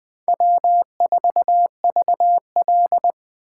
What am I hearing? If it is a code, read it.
Morse code, heard as a keyed tone, about 21 wpm, 700 Hz. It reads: W4VL
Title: USA Call Signs - 4 Characters - 4x Character Spacing - 20wpm